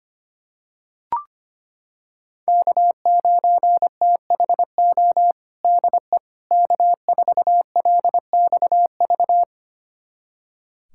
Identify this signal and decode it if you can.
Morse code, heard as a keyed tone, about 25 wpm, 700 Hz. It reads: K9THO DE K4LXV
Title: USA Callsign DE Callsign 25wpm